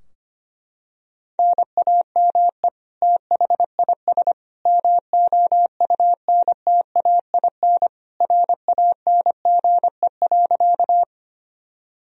Morse code, heard as a keyed tone, about 25 wpm, 700 Hz. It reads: NAME THIS MOUNTAIN RANGE.
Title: Sentences from Top 1000 Words - Review Part 4 - 25wpm